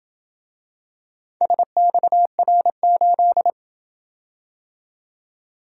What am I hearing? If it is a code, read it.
Morse code, heard as a keyed tone, about 27 wpm, 700 Hz. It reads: SXR8